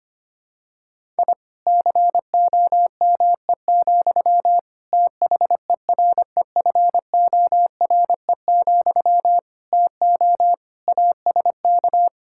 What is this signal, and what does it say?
Morse code, heard as a keyed tone, about 25 wpm, 700 Hz. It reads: I COME, THEREFORE, TO ASK
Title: Sentences from Top 1000 Words - Review Part 5 - 25wpm